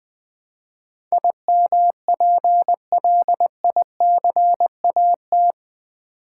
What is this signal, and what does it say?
Morse code, heard as a keyed tone, about 20 wpm, 700 Hz. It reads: IMPLICAT
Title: Build A Word - 4 Letters At A Time - 11-Letter Words - 20wpm